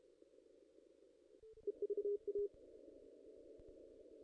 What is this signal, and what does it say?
Morse code, heard as a keyed tone, about 32 wpm, 410 Hz. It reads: EVA
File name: contests / ss-cw-k5tr-op-n5rz -2300